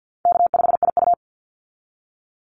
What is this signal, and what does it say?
Morse code, heard as a keyed tone, about 50 wpm, 700 Hz. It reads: K5IU
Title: Easy Call Signs Repeated 2x - 50wpm